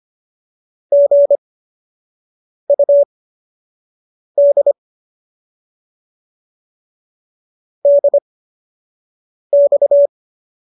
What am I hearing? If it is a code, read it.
Morse code, heard as a keyed tone, about 25 wpm, spaced as for 6 wpm, 570 Hz. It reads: GUD DX